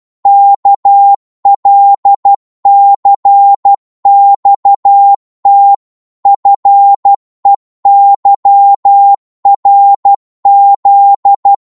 Morse code, heard as a keyed tone, about 12 wpm, 800 Hz. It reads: KLCXT FEYRZ
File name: Morse / ltr-12wpm